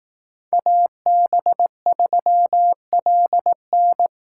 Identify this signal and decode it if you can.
Morse code, heard as a keyed tone, about 18 wpm, 700 Hz. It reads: AB3LN
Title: Speed Racing - USA 2x2 Call Signs 15wpm